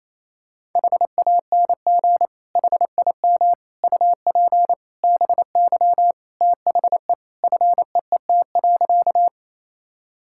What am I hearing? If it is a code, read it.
Morse code, heard as a keyed tone, about 28 wpm, 700 Hz. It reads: HANG HIM UP BY THE FEET.
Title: Sentences from Top 1600 Words 28wpm